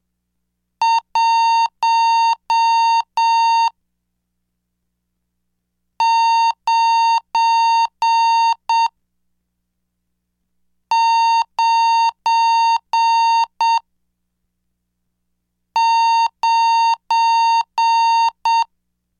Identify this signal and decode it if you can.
Morse code, heard as a keyed tone, about 7 wpm, spaced as for 3 wpm, 920 Hz. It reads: 1999
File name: cursos / CDestudo / 24 Track 24